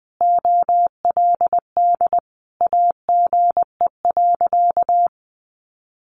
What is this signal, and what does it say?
Morse code, heard as a keyed tone, about 20 wpm, 700 Hz. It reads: OLD AGE.
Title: Sentences from Top 1300 Words 20wpm